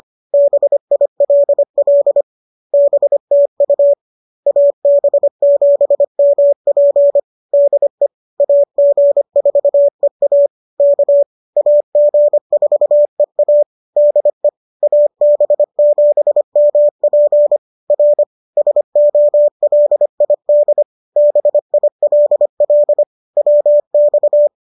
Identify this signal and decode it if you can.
Morse code, heard as a keyed tone, about 25 wpm, 590 Hz. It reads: BILL BTU AB7MP DE AG4EA K AG4EA DE AB7MP R SOLID BILL WX